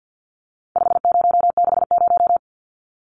Morse code, heard as a keyed tone, about 50 wpm, 700 Hz. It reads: 5060
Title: Numbers - 4-Digit 50wpm